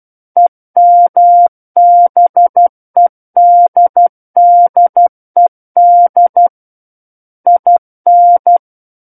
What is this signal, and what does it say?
Morse code, heard as a keyed tone, about 12 wpm, 700 Hz. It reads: EMBEDDED IN
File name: Book One / 03-Chapter 3 - 12 WPM